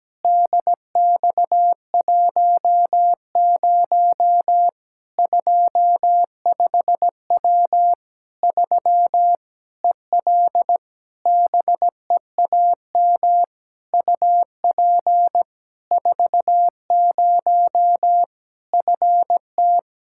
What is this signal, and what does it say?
Morse code, heard as a keyed tone, about 17 wpm, 700 Hz. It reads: DX10 25W 3 EL BEAM UP 40 FT
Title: Practice QSO - Exchange 2 - Info - 17wpm